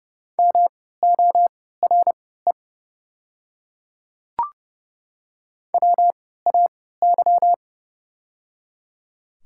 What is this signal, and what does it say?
Morse code, heard as a keyed tone, about 30 wpm, spaced as for 17 wpm, 700 Hz. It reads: MORE WAY
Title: 3x Character Spacing - Top 200 Words - 30wpm